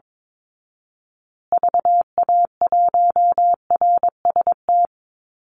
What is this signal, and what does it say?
Morse code, heard as a keyed tone, about 22 wpm, 690 Hz. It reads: VA1RST
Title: Call Signs Repeated 2x - 22wpm